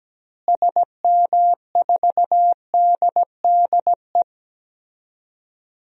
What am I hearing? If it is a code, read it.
Morse code, heard as a keyed tone, about 17 wpm, 700 Hz. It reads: SM4DDE
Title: DXCC-100 Call Signs - 6 Characters - Repeated 2x - 17wpm